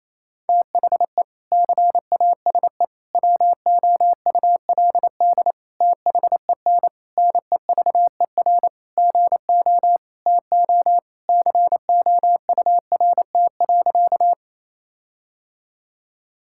Morse code, heard as a keyed tone, about 28 wpm, 700 Hz. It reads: THE CASE WOULD THEN NEVER GO TO COURT.
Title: Sentences from Top 1600 Words 28wpm